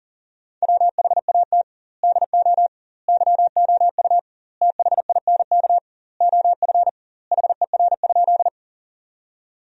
Morse code, heard as a keyed tone, about 40 wpm, 700 Hz. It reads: WHAT DO YOU THINK OF HER?
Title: 2x Word-Spacing - Sentences from Top 200 Words - 40wpm